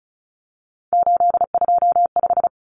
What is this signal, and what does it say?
Morse code, heard as a keyed tone, about 34 wpm, 700 Hz. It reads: 825
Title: Common Cut Numbers - 3-Digit 35wpm